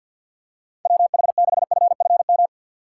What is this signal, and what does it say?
Morse code, heard as a keyed tone, about 51 wpm, 700 Hz. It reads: WH6FFM
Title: USA Call Signs - 3x Character Spacing - 50wpm